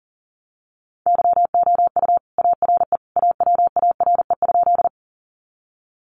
Morse code, heard as a keyed tone, about 40 wpm, 700 Hz. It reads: YOU ARE AWARE?
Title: Sentences from Top 1200 Words 40wpm